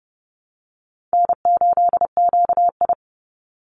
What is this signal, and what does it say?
Morse code, heard as a keyed tone, about 30 wpm, 700 Hz. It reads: N8QI
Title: USA Call Signs 30wpm